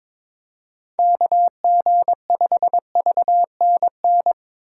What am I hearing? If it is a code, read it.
Morse code, heard as a keyed tone, about 26 wpm, 700 Hz. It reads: KG5VNN
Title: Speed Racing - USA 2x3 Call Signs 22wpm